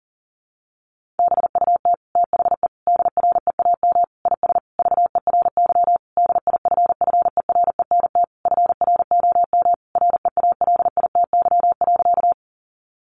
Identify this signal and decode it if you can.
Morse code, heard as a keyed tone, about 40 wpm, 700 Hz. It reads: BUT THE DREAM IS VERY DIFFERENT FROM REALITY.
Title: Sentences from Top 1500 Words 40wpm